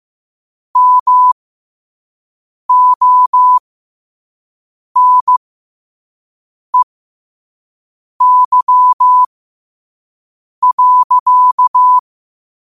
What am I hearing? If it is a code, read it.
Morse code, heard as a keyed tone, about 15 wpm, spaced as for 5 wpm, 1000 Hz. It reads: MONEY.
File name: qotdfiles / 5wpm